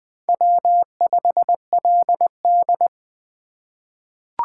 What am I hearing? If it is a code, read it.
Morse code, heard as a keyed tone, about 21 wpm, 700 Hz. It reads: W5LD
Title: USA Call Signs - 2x Character Spacing - 20wpm